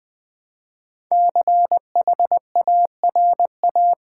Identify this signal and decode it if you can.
Morse code, heard as a keyed tone, about 20 wpm, 700 Hz. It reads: CHARA
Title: Build A Word - 5 Letters At A Time - 12-Letter Words - 20wpm